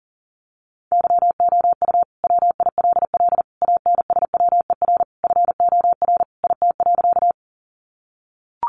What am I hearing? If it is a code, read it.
Morse code, heard as a keyed tone, about 40 wpm, 700 Hz. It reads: YOU WILL ANSWER FOR IT.
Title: Sentences from Top 300 Words 40wpm